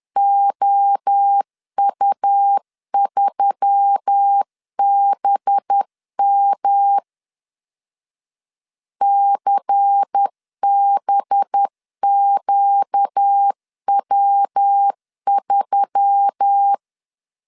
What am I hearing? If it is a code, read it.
Morse code, heard as a keyed tone, about 10 wpm, 800 Hz. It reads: OU3BM CBQW3
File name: CW / DET / 1 6 1500C S C 10wpm 4252